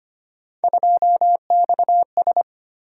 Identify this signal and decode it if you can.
Morse code, heard as a keyed tone, about 25 wpm, 700 Hz. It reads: 2XS
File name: Three Letter-Number Combinations - 25wpm